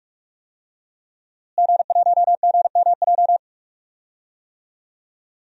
Morse code, heard as a keyed tone, about 44 wpm, 700 Hz. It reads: G1GMJ
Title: DXCC-100 Call Signs - 5 Characters - Repeated 2x - 45wpm